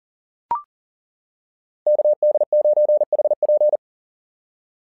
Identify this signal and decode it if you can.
Morse code, heard as a keyed tone, about 40 wpm, 600 Hz. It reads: KD9HP
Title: SOTA QSO Practice 40wpm